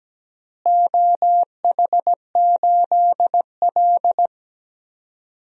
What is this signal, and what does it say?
Morse code, heard as a keyed tone, about 17 wpm, 700 Hz. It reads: OH8L
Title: DXCC-100 Call Signs - 4 Characters - Repeated 3x - 17wpm